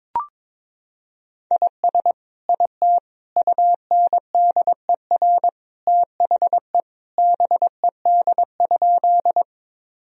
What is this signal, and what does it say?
Morse code, heard as a keyed tone, about 22 wpm, 700 Hz. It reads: IS IT UNDER THE BED?
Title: Sentences from Top 1000 Words - Review Part 1 - 22wpm